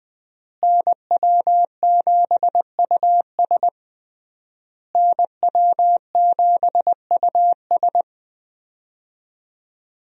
Morse code, heard as a keyed tone, about 20 wpm, 700 Hz. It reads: NW7US NW7US
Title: USA Call Signs - 5 Characters - Repeated 3x - 20wpm